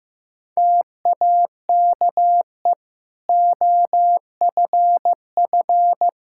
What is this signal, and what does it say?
Morse code, heard as a keyed tone, about 16 wpm, 700 Hz. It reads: TAKE OFF
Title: Sentences from Top 1600 Words 15wpm